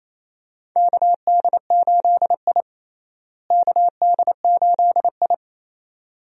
Morse code, heard as a keyed tone, about 28 wpm, 700 Hz. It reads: KD8I KD8I
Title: USA Call Signs - 4 Characters - Repeated 2x - 28wpm